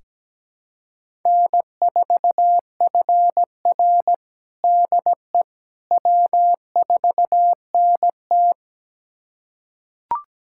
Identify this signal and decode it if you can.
Morse code, heard as a keyed tone, about 17 wpm, 700 Hz. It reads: N4FR DE W4NT
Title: USA Callsign DE Callsign - 4 Characters - 17wpm